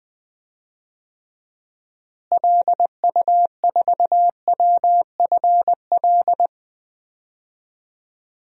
Morse code, heard as a keyed tone, about 20 wpm, 700 Hz. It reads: LU4WFL
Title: DXCC-100 Call Signs - 6 Characters - Repeated 2x - 20wpm